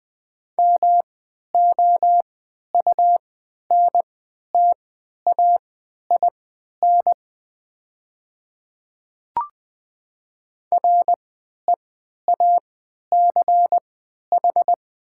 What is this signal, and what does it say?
Morse code, heard as a keyed tone, about 21 wpm, spaced as for 12 wpm, 700 Hz. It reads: MOUNTAIN REACH
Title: 3x Character Spacing - Top 500 Words - 20wpm